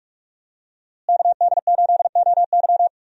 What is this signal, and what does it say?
Morse code, heard as a keyed tone, about 46 wpm, 700 Hz. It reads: KD8OY
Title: USA Call Signs - 5 Characters - Repeated 3x - 45wpm